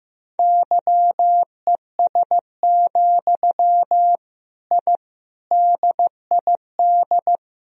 Morse code, heard as a keyed tone, about 15 wpm, 700 Hz. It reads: YES, I DID
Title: Sentences from Top 1000 Words - Review Part 2 - 15wpm